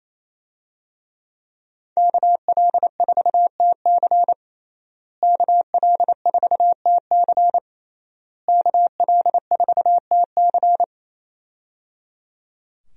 Morse code, heard as a keyed tone, about 28 wpm, 700 Hz. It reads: KL4TC KL4TC KL4TC
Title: USA Call Signs - 5 Characters - Repeated 3x - 28wpm